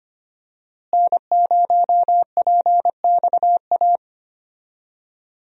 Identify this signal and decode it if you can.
Morse code, heard as a keyed tone, about 25 wpm, 700 Hz. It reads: N0PXA
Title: USA Call Signs - 5 Characters - Repeated 3x - 25wpm